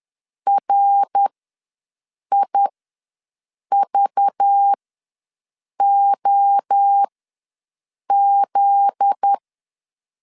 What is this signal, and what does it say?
Morse code, heard as a keyed tone, about 10 wpm, spaced as for 6 wpm, 800 Hz. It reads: RIVOZ
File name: LEZIONI / 7 GF9Z / 7 GF9Z ws 10wpm 3919